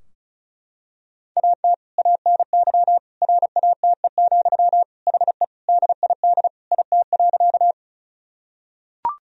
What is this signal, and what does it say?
Morse code, heard as a keyed tone, about 36 wpm, 700 Hz. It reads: AT ANY RATE, HE DID IT.
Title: Sentences from Top 1000 Words - Review Part 4 - 35wpm